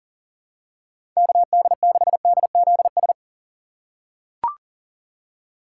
Morse code, heard as a keyed tone, about 40 wpm, 700 Hz. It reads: KD6DZS
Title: USA Call Signs - 6 Characters - 3x Character Spacing - 40wpm